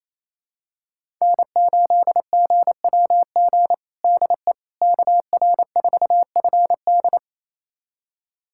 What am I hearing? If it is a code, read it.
Morse code, heard as a keyed tone, about 27 wpm, 700 Hz. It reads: N8GWG DE KR4FD
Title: USA Callsign DE Callsign - 5 Characters - 28wpm